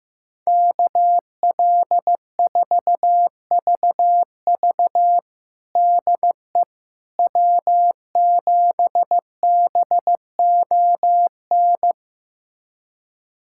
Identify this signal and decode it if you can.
Morse code, heard as a keyed tone, about 15 wpm, 700 Hz. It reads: KL4VV DE W7BON
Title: USA Callsign DE Callsign - 5 Characters - 15wpm